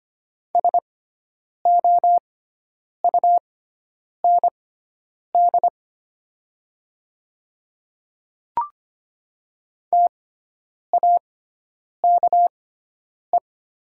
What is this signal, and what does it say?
Morse code, heard as a keyed tone, about 25 wpm, spaced as for 9 wpm, 700 Hz. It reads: SOUND TAKE
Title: 6x Character Spacing - Top 500 Words - 25wpm